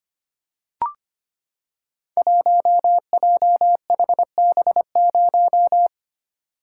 Morse code, heard as a keyed tone, about 25 wpm, 700 Hz. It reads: 1JHB0
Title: Code Groups - 5 Letter:Numbers - 25wpm